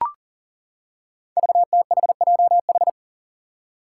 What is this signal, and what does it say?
Morse code, heard as a keyed tone, about 39 wpm, 700 Hz. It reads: VTHJH
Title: Five Challenging Letter-Number Combinations 40wpm